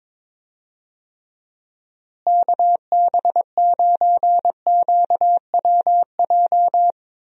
Morse code, heard as a keyed tone, about 22 wpm, 700 Hz. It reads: KB9QWJ
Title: USA Call Signs - 6 Characters - Repeated 2x - 22wpm